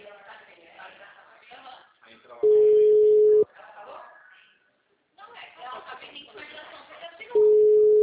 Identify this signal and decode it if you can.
Ringback tone (Europe, 425 Hz), with speech.